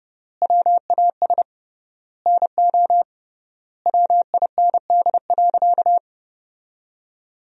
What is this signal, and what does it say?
Morse code, heard as a keyed tone, about 32 wpm, 700 Hz. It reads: WAS NO WIND.
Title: 3x Word-Spacing - Sentences from Top 400 Words - 30wpm